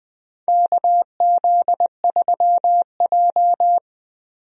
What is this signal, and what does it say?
Morse code, heard as a keyed tone, about 21 wpm, 700 Hz. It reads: KZ3J
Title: USA Call Signs - 4 Characters - Repeated 3x - 20wpm